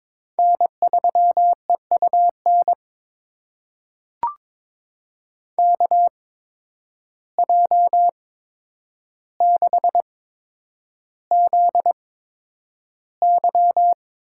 Morse code, heard as a keyed tone, about 22 wpm, spaced as for 9 wpm, 700 Hz. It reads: N3EUN KJ6ZY
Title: USA Call Signs - 5 Characters - 8x Character Spacing - 22wpm